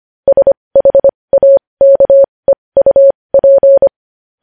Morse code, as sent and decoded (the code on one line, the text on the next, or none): ... .... .- -.- . ..- .--.
SHAKEUP